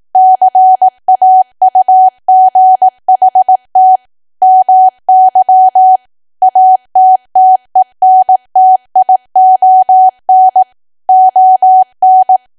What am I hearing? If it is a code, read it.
Morse code, heard as a keyed tone, about 18 wpm, 750 Hz. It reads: CAUGHT MY ATTENTION ON
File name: cwa / W1AW-Level2 / 18WPM